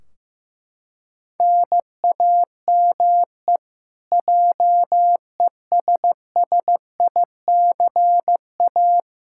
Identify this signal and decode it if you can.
Morse code, heard as a keyed tone, about 15 wpm, 700 Hz. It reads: NAME JESSICA
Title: Practice QSO - Exchange 1 - Beginning - 15wpm